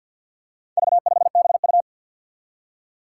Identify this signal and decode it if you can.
Morse code, heard as a keyed tone, about 47 wpm, 700 Hz. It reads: V5BU
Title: Four Challenging Letter-Number Combinations 50wpm